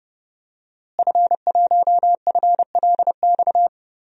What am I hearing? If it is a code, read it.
Morse code, heard as a keyed tone, about 30 wpm, 700 Hz. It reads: F1FLX
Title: DXCC-100 Call Signs - 5 Characters - Repeated 3x - 30wpm